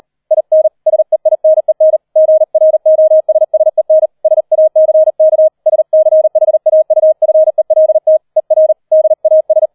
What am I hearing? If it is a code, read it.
Morse code, heard as a keyed tone, about 38 wpm, 630 Hz. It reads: IN SEINEN GROSSEN SACK SCHAUFELT ER DAS